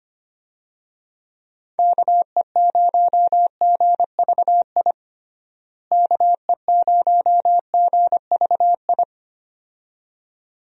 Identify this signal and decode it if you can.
Morse code, heard as a keyed tone, about 25 wpm, 700 Hz. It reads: KE0GVI KE0GVI
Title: USA Call Signs - 6 Characters - Repeated 2x - 25wpm